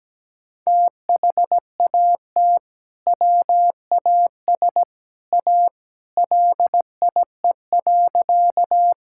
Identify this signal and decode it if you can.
Morse code, heard as a keyed tone, about 17 wpm, 700 Hz. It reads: THAT WAS A LIE.
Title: Sentences from Top 1600 Words 17wpm